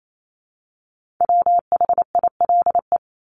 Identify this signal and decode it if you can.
Morse code, heard as a keyed tone, about 28 wpm, 700 Hz. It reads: WHILE
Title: Top 300 Words Repeated 28wpm